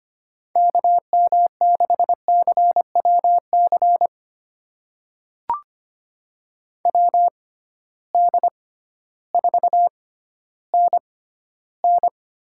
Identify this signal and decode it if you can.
Morse code, heard as a keyed tone, about 25 wpm, spaced as for 13 wpm, 700 Hz. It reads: KM6CWC WD4NN
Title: USA Call Signs - 6 Characters - 6x Character Spacing - 25wpm